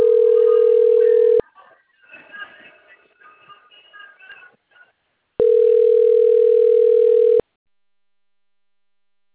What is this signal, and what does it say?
Ringback tone (North America), 440 + 480 Hz.
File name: rg-1600-unknown- - - .44905